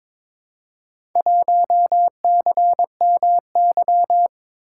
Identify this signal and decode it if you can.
Morse code, heard as a keyed tone, about 22 wpm, 700 Hz. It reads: 1CMY
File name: practice / 22wpm / Four Letter-Number Combinations - 22wpm